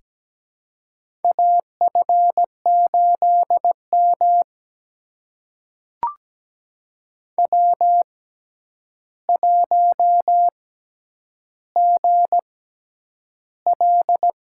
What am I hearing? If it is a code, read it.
Morse code, heard as a keyed tone, about 17 wpm, spaced as for 9 wpm, 700 Hz. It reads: AF8M W1GL
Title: USA Call Signs - 4 Characters - 6x Character Spacing - 17wpm